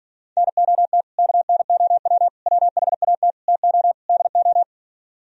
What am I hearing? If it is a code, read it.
Morse code, heard as a keyed tone, about 43 wpm, 700 Hz. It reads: NOT KNOW WHAT TO DO
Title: Speed-Racing - Sets of 5 Words 35wpm